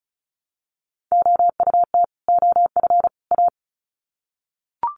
Morse code, heard as a keyed tone, about 34 wpm, 700 Hz. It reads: OUT OF A
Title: Sets of 3 Words 35wpm